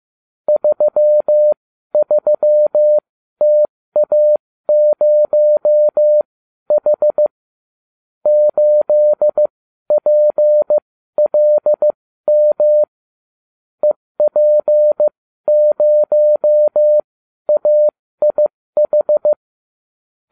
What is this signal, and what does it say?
Morse code, heard as a keyed tone, about 15 wpm, spaced as for 12 wpm, 600 Hz. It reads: 33TA0H 8PLM EP0AIH